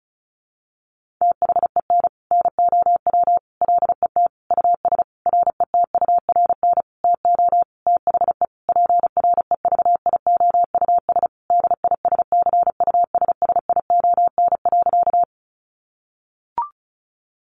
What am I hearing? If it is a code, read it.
Morse code, heard as a keyed tone, about 34 wpm, 700 Hz. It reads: THEN NOW LET US RETURN TO THE PREVIOUS DISCUSSION.
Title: Sentences from Top 1500 Words 35wpm